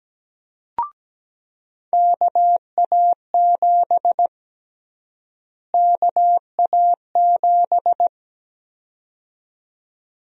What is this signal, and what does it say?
Morse code, heard as a keyed tone, about 17 wpm, 700 Hz. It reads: KA7 KA7
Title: USA Call Sign Prefixes - Repeated 2x - 17wpm